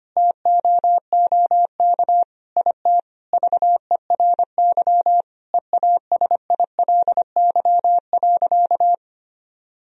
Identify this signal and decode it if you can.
Morse code, heard as a keyed tone, about 25 wpm, 700 Hz. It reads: TOOK IT VERY EASILY.
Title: Sentences from Top 1000 Words - Review Part 1 - 25wpm